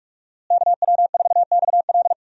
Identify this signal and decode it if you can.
Morse code, heard as a keyed tone, about 41 wpm, 700 Hz. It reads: KW4XL
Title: USA Call Signs - 5 Characters - Repeated 3x - 45wpm